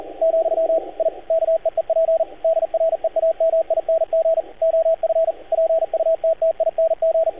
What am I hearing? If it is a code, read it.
Morse code, heard as a keyed tone, about 38 wpm, 650 Hz, with radio noise. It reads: , I KEEP DREAMING OF PUTTING
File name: archived / CRSnet